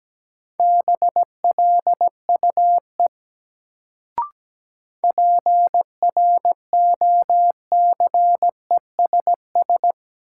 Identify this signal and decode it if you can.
Morse code, heard as a keyed tone, about 17 wpm, 700 Hz. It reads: BLUE PROCESS
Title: Sentences from Top 1000 Words - Review Part 3 - New Word Drill - 17wpm